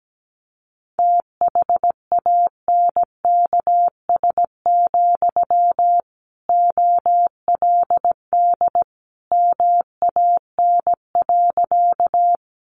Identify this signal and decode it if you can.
Morse code, heard as a keyed tone, about 17 wpm, 700 Hz. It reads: THANKS, OLD MAN.
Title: Sentences from Top 600 Words 17wpm